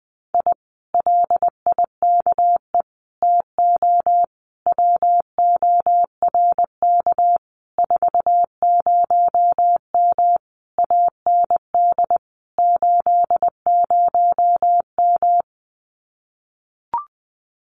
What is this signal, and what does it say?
Morse code, heard as a keyed tone, about 20 wpm, 700 Hz. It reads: I LIKE TO WORK 40M AND 80M
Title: Common QSO Phrases Encore 20wpm